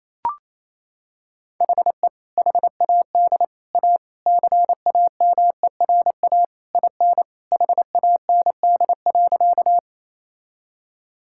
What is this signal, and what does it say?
Morse code, heard as a keyed tone, about 28 wpm, 700 Hz. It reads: HE HAD A CAMERA IN HAND.
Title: Sentences from Top 1000 Words - Review Part 5 - 28wpm